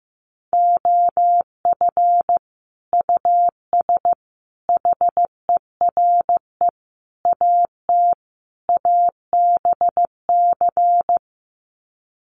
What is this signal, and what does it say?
Morse code, heard as a keyed tone, about 15 wpm, 700 Hz. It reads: OF US HERE AT ABC
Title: Sets of 5 Words 15wpm